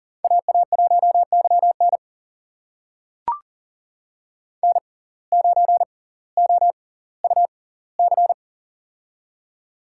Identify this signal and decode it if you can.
Morse code, heard as a keyed tone, about 39 wpm, spaced as for 20 wpm, 700 Hz. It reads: AA1YN N9OUC
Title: USA Call Signs - 5 Characters - 6x Character Spacing - 40wpm